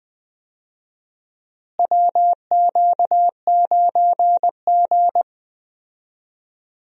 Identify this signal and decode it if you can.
Morse code, heard as a keyed tone, about 20 wpm, 700 Hz. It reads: WQ9G
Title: USA Call Signs - 4 Characters - Repeated 3x - 20wpm